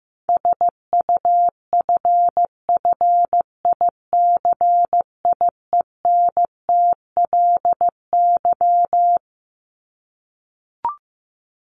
Morse code, heard as a keyed tone, about 15 wpm, 700 Hz. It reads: SUFFICIENTLY
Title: LY Suffix - Top 100 Words with Base Words 15wpm